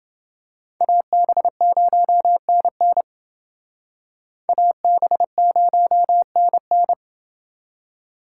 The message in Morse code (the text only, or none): AB0NN AB0NN